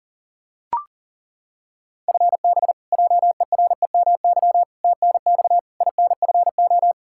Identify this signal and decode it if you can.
Morse code, heard as a keyed tone, about 40 wpm, 700 Hz. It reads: FB JEREMY TNX INFO
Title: Practice QSO - Exchange 3 - More Info - 40wpm